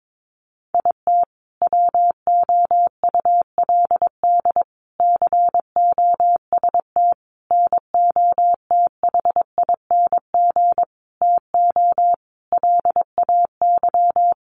Morse code, heard as a keyed tone, about 22 wpm, 700 Hz. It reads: IT WOULD COST NOTHING TO LAY
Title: Sentences from Top 800 Words 22wpm